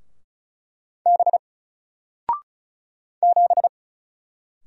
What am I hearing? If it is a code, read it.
Morse code, heard as a keyed tone, about 36 wpm, 700 Hz. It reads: B 7
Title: Single Letter-Number 35wpm